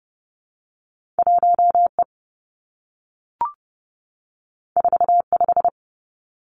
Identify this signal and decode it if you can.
Morse code, heard as a keyed tone, about 30 wpm, 700 Hz. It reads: 1 E 4 5
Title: Two Challenging Letter-Number Combinations 30wpm